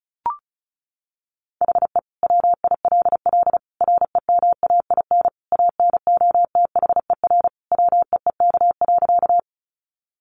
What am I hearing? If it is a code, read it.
Morse code, heard as a keyed tone, about 34 wpm, 700 Hz. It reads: HE WILL REMAIN ANOTHER WEEK.
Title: Sentences from Top 1200 Words 35wpm